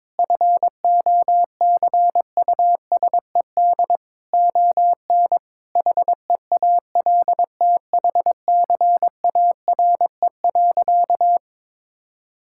Morse code, heard as a keyed tone, about 22 wpm, 700 Hz. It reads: FOCUSED ON HEALTHCARE.
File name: Sentences with 7-Word Phrases - 22wpm